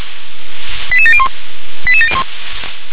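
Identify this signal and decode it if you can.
Five-tone selective call (ZVEI), heard as quick runs of five tones, tones of about 70 ms. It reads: 88701 88701
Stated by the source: ZVEI five-tone selective call received on a software-defined radio.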